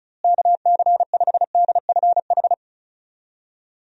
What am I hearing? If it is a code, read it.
Morse code, heard as a keyed tone, about 36 wpm, 700 Hz. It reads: KC5DFH
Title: USA Call Signs - 6 Characters - Repeated 3x - 35wpm